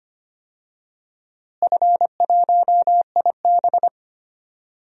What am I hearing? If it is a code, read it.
Morse code, heard as a keyed tone, about 25 wpm, 700 Hz. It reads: F1IB
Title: DXCC-100 Call Signs - 4 Characters - Repeated 3x - 25wpm